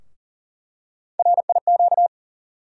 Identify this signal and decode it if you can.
Morse code, heard as a keyed tone, about 38 wpm, 700 Hz. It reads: RIQ
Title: USA Call Sign Postfixes - 3 Letters 40wpm